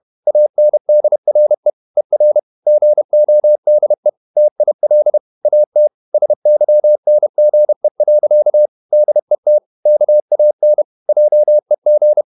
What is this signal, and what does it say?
Morse code, heard as a keyed tone, about 31 wpm, 600 Hz. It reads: ANDRE ER GODE TIL AT SYNGE. DET KAN JEG
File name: Ferie QRQ 31wpm